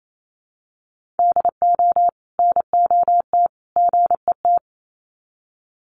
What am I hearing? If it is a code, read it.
Morse code, heard as a keyed tone, about 28 wpm, 700 Hz. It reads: DO NOT GET
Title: Sets of 3 Words 28wpm